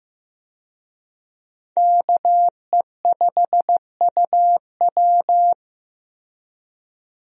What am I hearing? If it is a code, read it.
Morse code, heard as a keyed tone, about 15 wpm, 700 Hz. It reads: KE5UW
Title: USA Call Signs - 5 Characters - Repeated 2x - 15wpm